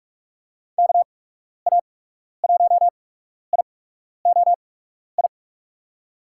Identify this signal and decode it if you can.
Morse code, heard as a keyed tone, about 42 wpm, spaced as for 12 wpm, 700 Hz. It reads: KA1IOI
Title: USA Call Signs - 6 Characters - 8x Character Spacing - 45wpm